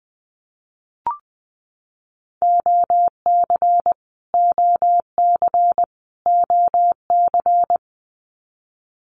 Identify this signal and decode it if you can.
Morse code, heard as a keyed tone, about 20 wpm, 700 Hz. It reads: OC OC OC
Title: Abbreviations Repeated 3x - 20wpm